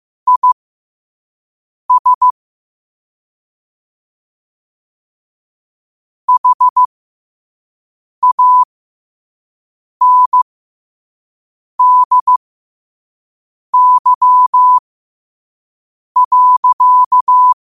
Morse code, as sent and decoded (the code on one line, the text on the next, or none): .. ... / .... .- -. -.. -.-- .-.-.-
IS HANDY.